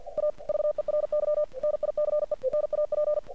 Morse code, heard as a keyed tone, about 48 wpm, 600 Hz. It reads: A VERY NICE NAP
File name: archived / CRSnet